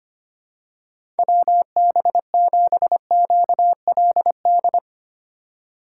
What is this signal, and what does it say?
Morse code, heard as a keyed tone, about 25 wpm, 700 Hz. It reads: WB7QLD
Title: USA Call Signs - 6 Characters - 3x Character Spacing - 25wpm